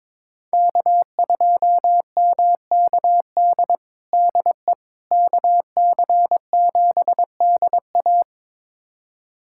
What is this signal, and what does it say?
Morse code, heard as a keyed tone, about 22 wpm, 700 Hz. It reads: K2MKD DE KC7DA
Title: USA Callsign DE Callsign - 5 Characters - 22wpm